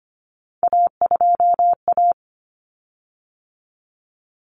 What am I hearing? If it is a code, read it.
Morse code, heard as a keyed tone, about 25 wpm, 700 Hz. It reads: A2A
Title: Common Cut Numbers - 3-Digit 25wpm